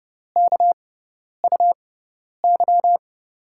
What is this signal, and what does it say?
Morse code, heard as a keyed tone, about 30 wpm, spaced as for 10 wpm, 700 Hz. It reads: KUY
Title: USA Call Signs - 6 Characters - 6x Character Spacing - 30wpm